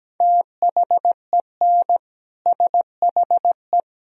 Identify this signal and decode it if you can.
Morse code, heard as a keyed tone, about 17 wpm, 700 Hz. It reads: THEN SHE
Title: Sentences from Top 400 Words and Abbreviations 17wpm